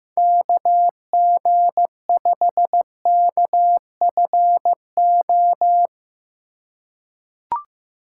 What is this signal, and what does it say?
Morse code, heard as a keyed tone, about 15 wpm, 700 Hz. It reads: KG5KFO
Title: USA Call Signs - 6 Characters - 3x Character Spacing - 15wpm